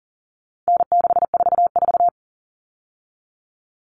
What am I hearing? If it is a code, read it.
Morse code, heard as a keyed tone, about 40 wpm, 700 Hz. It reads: N644
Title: All Cut Numbers - 4-Digit 40wpm